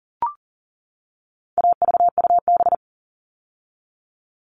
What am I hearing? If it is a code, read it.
Morse code, heard as a keyed tone, about 40 wpm, 700 Hz. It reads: AVUB
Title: All Cut Numbers - 4-Digit 40wpm